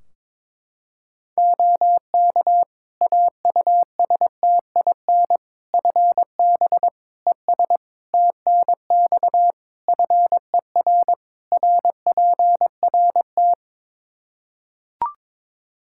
Morse code, heard as a keyed tone, about 22 wpm, 700 Hz. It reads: OK AUSTIN FB ES TNX FER RPRT
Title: Practice QSO - Exchange 2 - Info - 22wpm